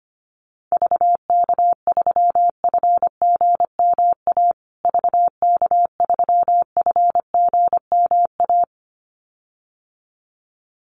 Morse code, heard as a keyed tone, about 25 wpm, 700 Hz. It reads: VK3FGMA VK3FGMA
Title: USA Call Signs Repeated 2x 25wpm